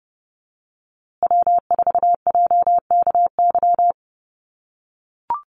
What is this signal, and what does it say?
Morse code, heard as a keyed tone, about 30 wpm, 700 Hz. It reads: W4JKY
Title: USA Call Signs Repeated 2x 30wpm